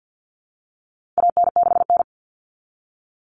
Morse code, heard as a keyed tone, about 50 wpm, 700 Hz. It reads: AN6N
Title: Common Cut Numbers - 4-Digit 50wpm